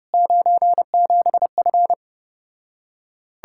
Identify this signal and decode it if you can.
Morse code, heard as a keyed tone, about 33 wpm, 700 Hz. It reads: 97F